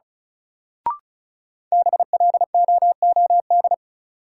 Morse code, heard as a keyed tone, about 36 wpm, 700 Hz. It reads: BLOOD
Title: Sentences from Top 1000 Words - Review Part 2 - New Word Drill - 35wpm